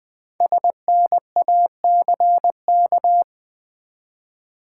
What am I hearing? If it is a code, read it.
Morse code, heard as a keyed tone, about 20 wpm, 700 Hz. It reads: SNACK
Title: Easy Words with Hints - 20wpm